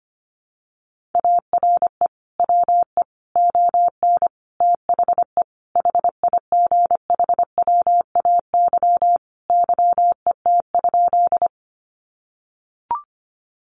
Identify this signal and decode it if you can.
Morse code, heard as a keyed tone, about 25 wpm, 700 Hz. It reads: ARE WE ON THE HIGHWAY YET?
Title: Sentences from Top 1400 Words 25wpm